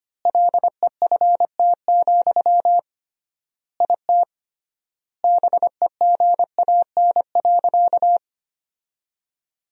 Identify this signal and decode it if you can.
Morse code, heard as a keyed tone, about 25 wpm, 700 Hz. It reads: LEFT, IT BEGAN.
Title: 3x Word-Spacing - Sentences from Top 400 Words - 25wpm